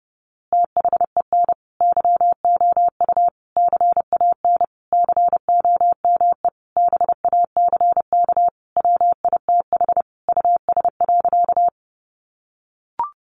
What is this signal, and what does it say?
Morse code, heard as a keyed tone, about 30 wpm, 700 Hz. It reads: THEN YOU CAN COME BACK WITH US.
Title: A Sound Of Thunder 30wpm